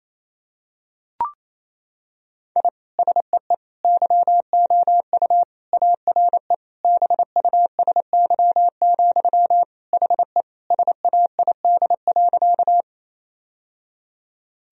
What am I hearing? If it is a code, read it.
Morse code, heard as a keyed tone, about 28 wpm, 700 Hz. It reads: I SEE YOU ARE BUSY, HE SAID.
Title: Sentences from Top 1000 Words - Review Part 3 - 28wpm